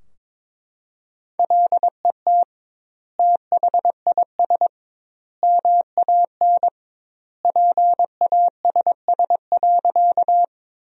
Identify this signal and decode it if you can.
Morse code, heard as a keyed tone, about 22 wpm, 700 Hz. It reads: LET THIS MAN PASS.
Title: 2x Word-Spacing - Sentences from Top 400 Words - 22wpm